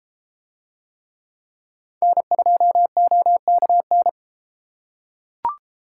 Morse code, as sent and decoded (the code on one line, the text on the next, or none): -. ..--- --- -.- -.
N2OKN